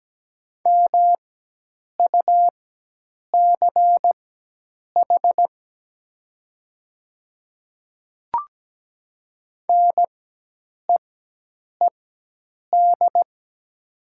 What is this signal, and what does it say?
Morse code, heard as a keyed tone, about 17 wpm, spaced as for 8 wpm, 700 Hz. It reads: MUCH NEED
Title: 4x Character Spacing - Top 200 Words - 17wpm